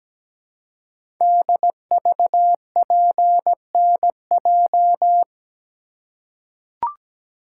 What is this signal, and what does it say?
Morse code, heard as a keyed tone, about 17 wpm, 700 Hz. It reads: DVPNJ
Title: Code Groups - 5 Letters - 17wpm